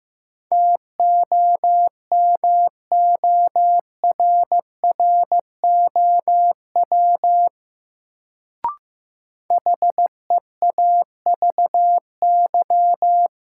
Morse code, heard as a keyed tone, about 15 wpm, 700 Hz. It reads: TOMORROW HEAVY
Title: Sentences from Top 1100 Words - New Word Drill Alt - 15wpm